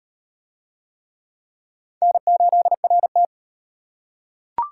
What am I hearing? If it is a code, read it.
Morse code, heard as a keyed tone, about 35 wpm, 700 Hz. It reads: N8RT